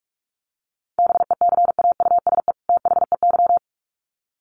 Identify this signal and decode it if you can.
Morse code, heard as a keyed tone, about 44 wpm, 700 Hz. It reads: BECAUSE THEY
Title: Sets of 2 Words 45wpm